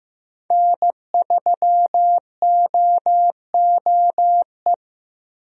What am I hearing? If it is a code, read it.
Morse code, heard as a keyed tone, about 16 wpm, 700 Hz. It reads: N3OOE
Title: USA Call Signs - 5 Characters - Repeated 2x - 15wpm